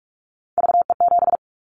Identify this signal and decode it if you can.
Morse code, heard as a keyed tone, about 46 wpm, 690 Hz. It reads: VE7
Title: Callsign Prefixes for Top 50 DXCC - 45wpm